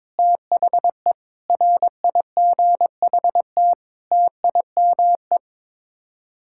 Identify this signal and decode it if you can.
Morse code, heard as a keyed tone, about 22 wpm, 700 Hz. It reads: THE RIGHT TIME
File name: Sets of 9 Words - 22wpm